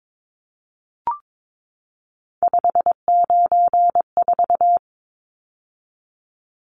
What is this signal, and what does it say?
Morse code, heard as a keyed tone, about 22 wpm, 700 Hz. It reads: 594
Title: Numbers - 3-Digit 22wpm